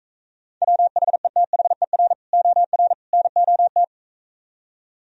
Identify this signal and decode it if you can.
Morse code, heard as a keyed tone, about 45 wpm, 700 Hz. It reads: WHETHER OR NOT